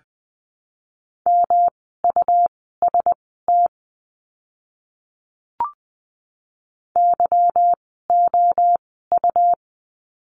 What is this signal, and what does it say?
Morse code, heard as a keyed tone, about 20 wpm, spaced as for 15 wpm, 700 Hz. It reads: MUST YOU
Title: 237 Lesson 38f - Letter q - Words 20wpm